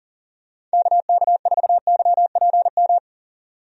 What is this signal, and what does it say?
Morse code, heard as a keyed tone, about 40 wpm, 700 Hz. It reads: KK4YPM
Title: USA Call Signs - 6 Characters - 40wpm